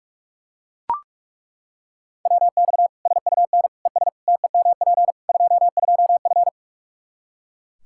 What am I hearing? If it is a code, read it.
Morse code, heard as a keyed tone, about 41 wpm, 700 Hz. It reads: WX SUN ES TEMP 22F
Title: Practice QSO - Exchange 2 - Info - 45wpm